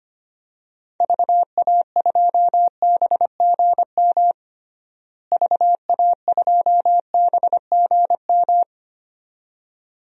Morse code, heard as a keyed tone, about 25 wpm, 700 Hz. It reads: VA2BGM VA2BGM
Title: DXCC-100 Call Signs - 6 Characters - Repeated 2x - 25wpm